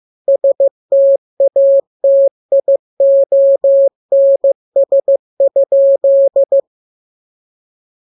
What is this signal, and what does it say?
Morse code, heard as a keyed tone, about 15 wpm, 550 Hz. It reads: STATIONS?
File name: jokes / joke 11 15 0000